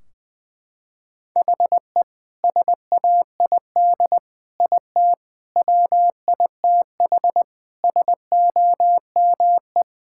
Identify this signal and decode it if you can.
Morse code, heard as a keyed tone, about 21 wpm, 700 Hz. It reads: HE SAID IT WITH SOME
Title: Sentences from Top 1600 Words 20wpm